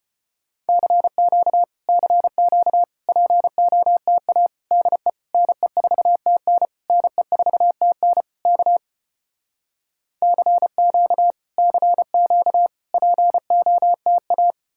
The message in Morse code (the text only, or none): CQ CQ POTA DE NE4TN NE4TN K CQ CQ POTA